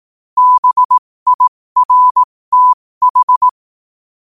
Morse code, heard as a keyed tone, about 18 wpm, spaced as for 15 wpm, 1000 Hz. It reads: BIRTH
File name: qotdfiles / 15wpm